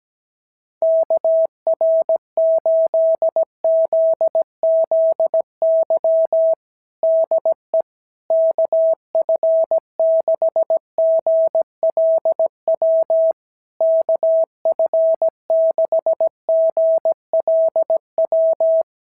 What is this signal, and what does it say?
Morse code, heard as a keyed tone, about 17 wpm, 650 Hz. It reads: KR8ZZY DE KF6GLW KF6GLW
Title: SOTA QSO Practice 17wpm